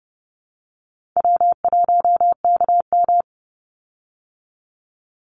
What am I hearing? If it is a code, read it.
Morse code, heard as a keyed tone, about 30 wpm, 700 Hz. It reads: W1KM
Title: Easy Call Signs Repeated 3x - 30wpm